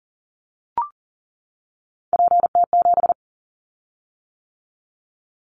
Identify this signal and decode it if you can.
Morse code, heard as a keyed tone, about 40 wpm, 700 Hz. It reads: PT7
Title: Callsign Prefixes for Top 50 DXCC - 40wpm